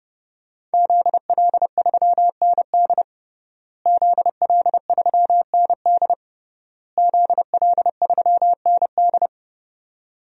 Morse code, heard as a keyed tone, about 30 wpm, 700 Hz. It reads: ZL3ND ZL3ND ZL3ND
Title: DXCC-100 Call Signs - 5 Characters - Repeated 3x - 30wpm